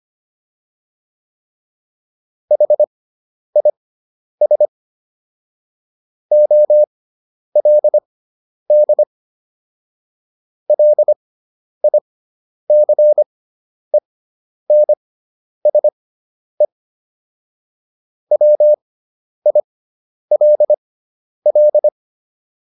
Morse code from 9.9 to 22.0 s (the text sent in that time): LICENSE WILL